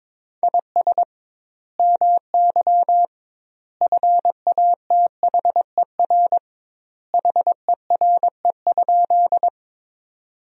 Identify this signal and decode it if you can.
Morse code, heard as a keyed tone, about 22 wpm, 700 Hz. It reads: IS MY FATHER HERE?
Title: 2x Word-Spacing - Sentences from Top 300 Words - 22wpm